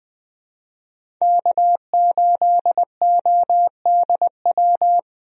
Morse code, heard as a keyed tone, about 20 wpm, 700 Hz. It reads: K8ODW
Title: USA Call Signs - 5 Characters - Repeated 3x - 20wpm